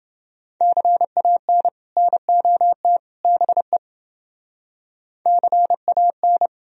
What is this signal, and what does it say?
Morse code, heard as a keyed tone, about 30 wpm, 700 Hz. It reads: CAN NOT BE CAN